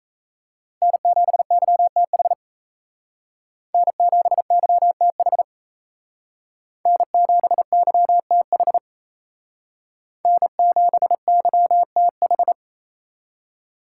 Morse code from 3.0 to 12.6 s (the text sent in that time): N7YTH N7YTH N7YTH